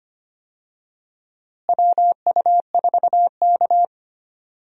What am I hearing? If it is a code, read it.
Morse code, heard as a keyed tone, about 25 wpm, 700 Hz. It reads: WU4K
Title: USA Call Signs - 4 Characters - Repeated 3x - 25wpm